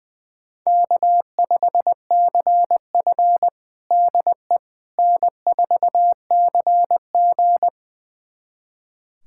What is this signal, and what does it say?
Morse code, heard as a keyed tone, about 20 wpm, 700 Hz. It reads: K5CF DE N4CG
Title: USA Callsign DE Callsign - 4 Characters - 20wpm